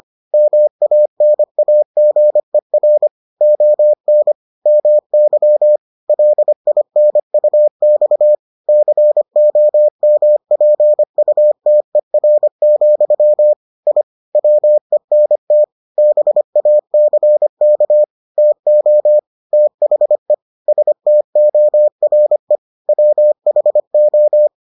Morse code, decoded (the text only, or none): MANAGER ON MY LINUX COMPUTER, I WENT BACK TO THE STORE WHO